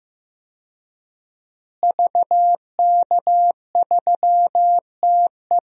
Morse code, heard as a keyed tone, about 15 wpm, 700 Hz. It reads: VK3TE
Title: DXCC-100 Call Signs - 5 Characters - Repeated 3x - 15wpm